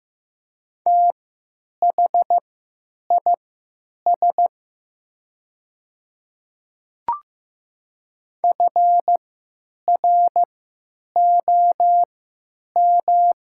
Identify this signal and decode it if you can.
Morse code, heard as a keyed tone, about 15 wpm, spaced as for 9 wpm, 700 Hz. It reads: THIS FROM
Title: 3x Character Spacing - Top 300 Words - 15wpm